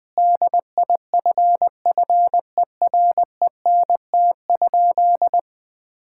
Morse code, heard as a keyed tone, about 20 wpm, 700 Hz. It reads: DIFFERENT?
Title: Sentences from Top 1600 Words 20wpm